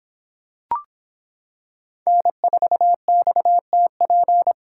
Morse code, heard as a keyed tone, about 26 wpm, 700 Hz. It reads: N4XTP